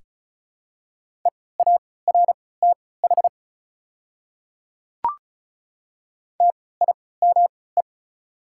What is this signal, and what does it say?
Morse code, heard as a keyed tone, about 36 wpm, spaced as for 20 wpm, 690 Hz. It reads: EARTH TIME
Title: 3x Character Spacing - Top 200 Words - 35wpm